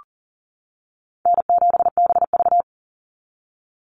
Morse code, heard as a keyed tone, about 40 wpm, 700 Hz. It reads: N7BV
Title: Call Signs - 40wpm